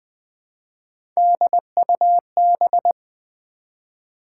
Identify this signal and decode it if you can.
Morse code, heard as a keyed tone, about 20 wpm, 700 Hz. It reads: DUB
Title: Build A Word - 1 Letter At A Time - 3-Letter Words - 20wpm